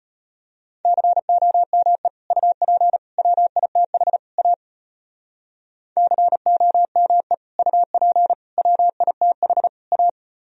Morse code, heard as a keyed tone, about 33 wpm, 700 Hz. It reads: COME UP WITH A COME UP WITH A